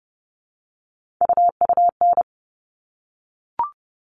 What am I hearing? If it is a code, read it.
Morse code, heard as a keyed tone, about 30 wpm, 700 Hz. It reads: UUN
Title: All Cut Numbers - 3-Digit 30wpm